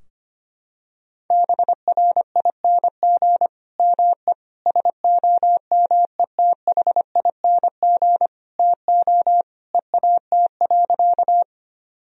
Morse code, heard as a keyed tone, about 25 wpm, 700 Hz. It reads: BRING ME SOMETHING TO EAT.
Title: Sentences from Top 1000 Words - Review Part 3 - 25wpm